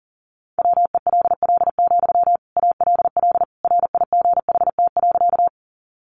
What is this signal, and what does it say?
Morse code, heard as a keyed tone, about 40 wpm, 700 Hz. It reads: WELL, ALL RIGHT.
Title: Sentences from Top 200 Words 40wpm